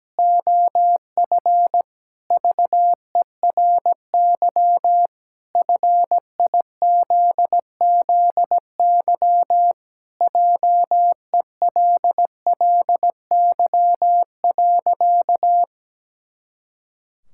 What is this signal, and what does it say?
Morse code, heard as a keyed tone, about 17 wpm, 700 Hz. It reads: OF VERY FIZZY JELLY.
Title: Pangram Challenge v2 - 17wpm